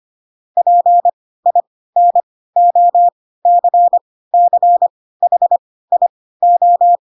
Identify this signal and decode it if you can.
Morse code, heard as a keyed tone, about 25 wpm, spaced as for 16 wpm, 700 Hz. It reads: PINOCCHIO